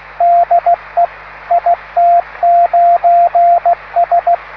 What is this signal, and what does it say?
Morse code, heard as a keyed tone, about 16 wpm, 690 Hz, with radio noise. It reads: DE IT9S